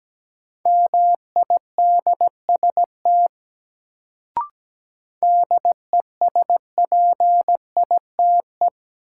Morse code, heard as a keyed tone, about 17 wpm, 700 Hz. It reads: MIDST DESPITE
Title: Sets of 4 Words - New Word Drill - 17wpm